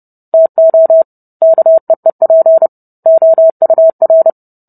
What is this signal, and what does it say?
Morse code, heard as a keyed tone, about 30 wpm, 650 Hz. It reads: TO KEEP OUR